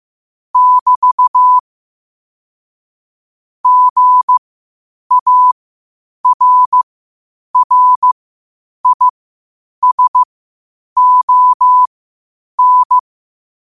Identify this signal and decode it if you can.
Morse code, heard as a keyed tone, about 15 wpm, spaced as for 8 wpm, 1000 Hz. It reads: <BT> GARRISON
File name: qotdfiles / 8wpm